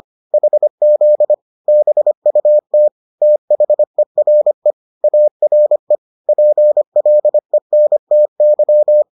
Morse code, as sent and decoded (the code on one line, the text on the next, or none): .... --.. / -... ..- - / - .... .-. / .- .-. / .--. .-.. -. - -.--
HZ BUT THERE ARE PLENTY